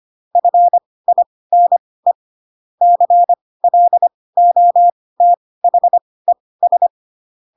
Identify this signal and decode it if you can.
Morse code, heard as a keyed tone, about 25 wpm, spaced as for 18 wpm, 700 Hz. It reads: FINE CLOTHES